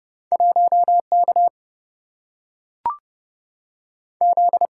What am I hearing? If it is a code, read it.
Morse code, heard as a keyed tone, about 29 wpm, 700 Hz. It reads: 1 K Z
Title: Single-Letter Single-Number 30wpm